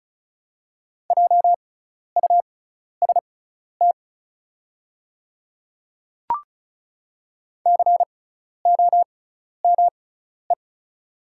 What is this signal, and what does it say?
Morse code, heard as a keyed tone, about 36 wpm, spaced as for 12 wpm, 690 Hz. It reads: JUST COME
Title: 6x Character Spacing - Top 300 Words - 35wpm